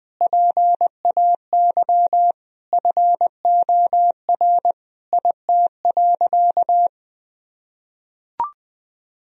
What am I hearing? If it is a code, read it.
Morse code, heard as a keyed tone, about 19 wpm, 700 Hz. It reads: PAY FOR IT.
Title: Sentences from Top 1000 Words - Review Part 3 - 20wpm